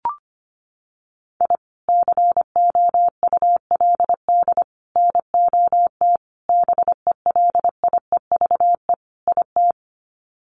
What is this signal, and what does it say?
Morse code, heard as a keyed tone, about 25 wpm, 700 Hz. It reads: I COULD NOT BELIEVE IT
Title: Sets of 5 Words Encore 25wpm